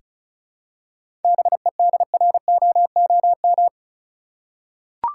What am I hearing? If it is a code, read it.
Morse code, heard as a keyed tone, about 34 wpm, 700 Hz. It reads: BEDROOM
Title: Seven Letter Words - 35wpm